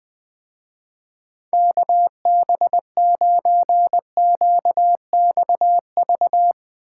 Morse code, heard as a keyed tone, about 22 wpm, 700 Hz. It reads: KB9QXV
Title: Speed Racing - USA 2x3 Call Signs 20wpm